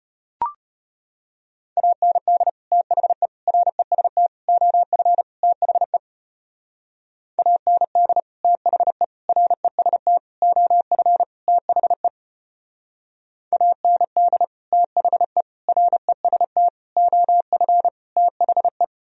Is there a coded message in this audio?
Yes: Morse code from 1.8 to 18.8 s, heard as a keyed tone, about 33 wpm, 700 Hz. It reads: AND THE REST OF THE AND THE REST OF THE AND THE REST OF THE